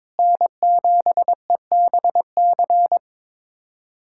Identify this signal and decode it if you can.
Morse code, heard as a keyed tone, about 22 wpm, 700 Hz. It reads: N7EBC
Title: USA Callsign DE Callsign - 5 Characters - 22wpm